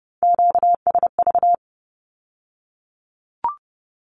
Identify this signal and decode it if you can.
Morse code, heard as a keyed tone, about 30 wpm, 690 Hz. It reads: QSV
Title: Q-Codes with Spoken Meaning - 30wpm